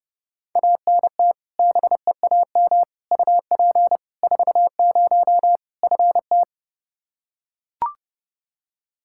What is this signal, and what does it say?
Morse code, heard as a keyed tone, about 30 wpm, 700 Hz. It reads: ANT BEAM UP 40 FT
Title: Practice QSO - Exchange 2 - Info - 30wpm